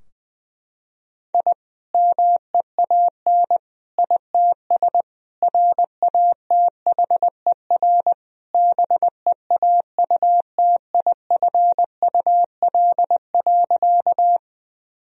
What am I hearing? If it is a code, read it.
Morse code, heard as a keyed tone, about 20 wpm, 700 Hz. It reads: I MEAN ITS RATHER BEAUTIFUL.
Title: Sentences from Top 1000 Words - Review Part 4 - 20wpm